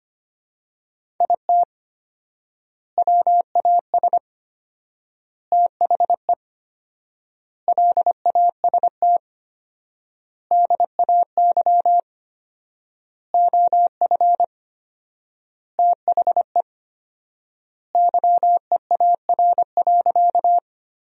Morse code, heard as a keyed tone, about 25 wpm, 700 Hz. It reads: IT WAS THE LAST DAY OF THE YEAR.
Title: 4x Word-Spacing - Sentences from Top 300 Words - 25wpm